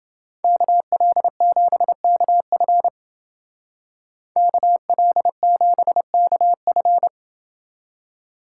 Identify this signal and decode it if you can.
Morse code, heard as a keyed tone, about 29 wpm, 700 Hz. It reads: KL7KF KL7KF